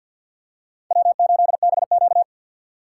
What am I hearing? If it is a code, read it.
Morse code, heard as a keyed tone, about 54 wpm, 700 Hz. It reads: W8BQ
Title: USA Call Signs - 4 Characters - 6x Character Spacing - 50wpm